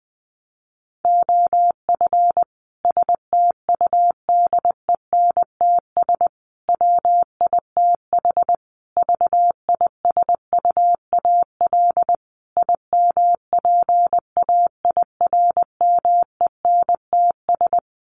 Morse code, heard as a keyed tone, about 20 wpm, 700 Hz. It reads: OF STUDENTS WITH VISUAL IMPAIRMENTS
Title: Sets of 5 Words Encore 20wpm